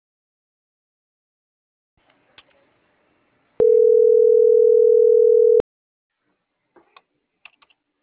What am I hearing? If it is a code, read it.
Ringback tone (North America), 440 + 480 Hz.